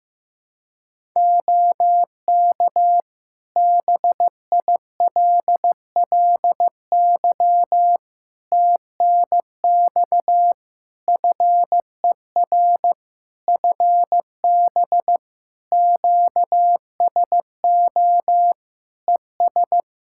Morse code, heard as a keyed tone, about 15 wpm, 700 Hz. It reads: OK BILLY TNX FER FB QSO ES
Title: Practice QSO - Exchange 4 - Ending - 15wpm